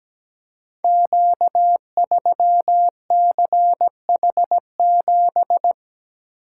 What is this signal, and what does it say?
Morse code, heard as a keyed tone, about 17 wpm, 700 Hz. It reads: Q3CH7
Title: Code Groups - 5 Letter:Numbers - 17wpm